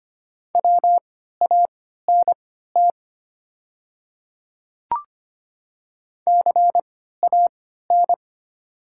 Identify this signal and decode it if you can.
Morse code, heard as a keyed tone, about 25 wpm, spaced as for 14 wpm, 690 Hz. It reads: WANT CAN
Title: 3x Character Spacing - Top 200 Words - 25wpm